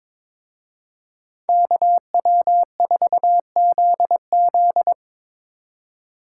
Morse code, heard as a keyed tone, about 25 wpm, 700 Hz. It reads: KW4ZZ
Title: Speed Racing - USA 2x2 Call Signs 22wpm